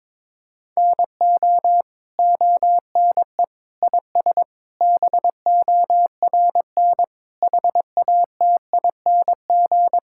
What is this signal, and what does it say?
Morse code, heard as a keyed tone, about 22 wpm, 700 Hz. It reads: NO ONE IS BORN HATING
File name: Sets of 9 Words - 22wpm